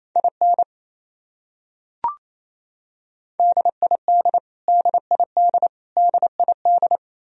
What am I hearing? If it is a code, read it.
Morse code, heard as a keyed tone, about 28 wpm, 700 Hz. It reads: IN DID DID DID
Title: Top 100 Words Repeated 28wpm